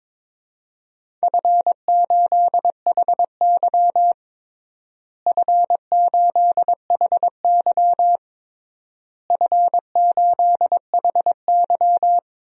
Morse code, heard as a keyed tone, about 22 wpm, 700 Hz. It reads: F8HY F8HY F8HY
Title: DXCC-100 Call Signs - 4 Characters - Repeated 3x - 22wpm